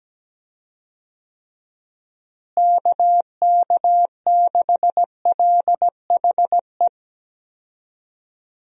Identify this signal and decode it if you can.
Morse code, heard as a keyed tone, about 17 wpm, 700 Hz. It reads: KK6LHE
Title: USA Call Signs - 6 Characters - Repeated 2x - 17wpm